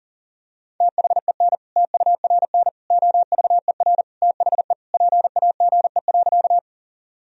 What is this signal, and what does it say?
Morse code, heard as a keyed tone, about 40 wpm, 700 Hz. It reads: THEN TURN OVER THE PAGE.
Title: Sentences from Top 1000 Words - Review Part 1 - 40wpm